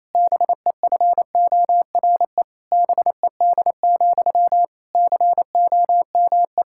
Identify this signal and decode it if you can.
Morse code, heard as a keyed tone, about 28 wpm, 700 Hz. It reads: BEFORE BED, COME
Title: Sentences from Top 1600 Words 28wpm